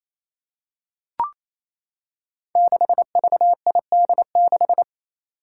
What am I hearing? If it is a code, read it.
Morse code, heard as a keyed tone, about 28 wpm, 700 Hz. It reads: 6VID6
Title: Five Challenging Letter-Number Combinations 28wpm